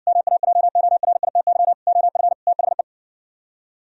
Morse code, heard as a keyed tone, about 60 wpm, 700 Hz. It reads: MAJORITY OF THE